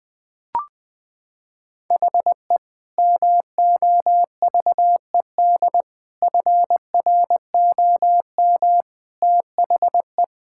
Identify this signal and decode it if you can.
Morse code, heard as a keyed tone, about 20 wpm, 700 Hz. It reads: HE MOVED FROM THE
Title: Sentences with 4-Word Phrases - 20wpm